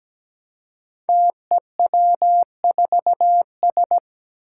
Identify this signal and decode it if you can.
Morse code, heard as a keyed tone, about 17 wpm, 700 Hz. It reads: TEW4S
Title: Five Challenging Letter-Number Combinations 17wpm